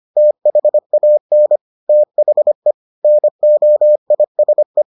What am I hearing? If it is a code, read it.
Morse code, heard as a keyed tone, about 25 wpm, 600 Hz. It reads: THAN THE NOISE